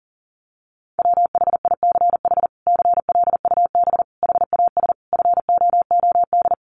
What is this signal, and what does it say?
Morse code, heard as a keyed tone, about 40 wpm, 700 Hz. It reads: WHICH CLUB HAS FOOD
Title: Sentences from Top 1200 Words 40wpm